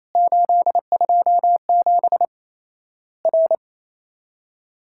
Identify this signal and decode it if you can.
Morse code, heard as a keyed tone, about 28 wpm, 700 Hz. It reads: 827 R
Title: Worked All Europe DX Contest - QTC Exchange - 28wpm